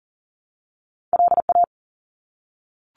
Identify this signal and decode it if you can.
Morse code, heard as a keyed tone, about 40 wpm, 700 Hz. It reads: LA
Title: ARRL Sections 40wpm